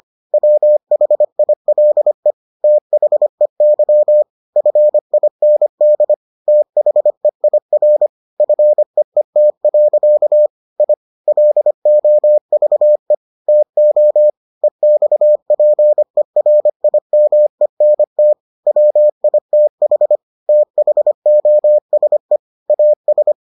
Morse code, heard as a keyed tone, about 25 wpm, 600 Hz. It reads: WHILE THEY FIND THEIR FEET. I LOVE TO EXPERIMENT WITH THOSE AS